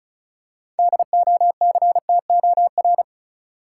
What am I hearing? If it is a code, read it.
Morse code, heard as a keyed tone, about 36 wpm, 700 Hz. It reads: DOCTOR
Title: Six Letter Words - 35wpm